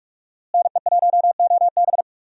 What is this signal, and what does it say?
Morse code, heard as a keyed tone, about 46 wpm, 700 Hz. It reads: NE1OB
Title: USA Call Signs - 5 Characters - Repeated 2x - 45wpm